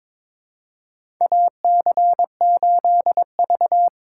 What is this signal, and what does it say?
Morse code, heard as a keyed tone, about 22 wpm, 700 Hz. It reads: AC8V
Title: USA Call Signs - 4 Characters - 2x Character Spacing - 22wpm